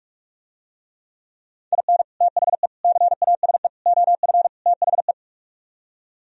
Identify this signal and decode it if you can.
Morse code, heard as a keyed tone, about 44 wpm, 700 Hz. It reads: IN THE CASE OF THE